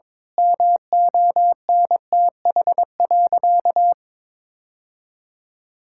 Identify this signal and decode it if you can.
Morse code, heard as a keyed tone, about 22 wpm, 700 Hz. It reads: MONTH.
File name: Difficult Sentences with Hints - 22wpm